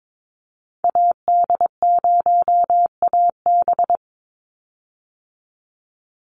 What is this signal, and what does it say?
Morse code, heard as a keyed tone, about 22 wpm, 700 Hz. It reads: AD0AB
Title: Call Signs Repeated 3x - 22wpm